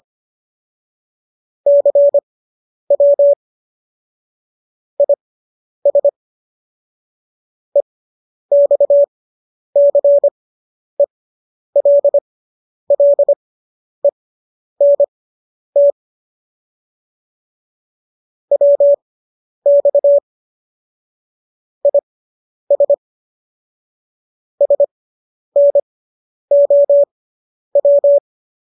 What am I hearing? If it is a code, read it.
Morse code, heard as a keyed tone, about 25 wpm, spaced as for 10 wpm, 570 Hz. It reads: CW IS EXCELLENT WX IS SNOW